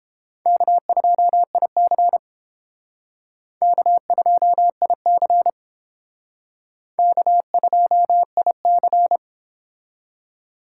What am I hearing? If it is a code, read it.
Morse code, heard as a keyed tone, about 28 wpm, 700 Hz. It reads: K2IC K2IC K2IC